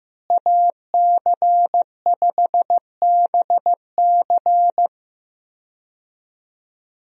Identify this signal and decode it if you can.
Morse code, heard as a keyed tone, about 15 wpm, 700 Hz. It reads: AC5BC
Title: USA Call Signs - 5 Characters - Repeated 2x - 15wpm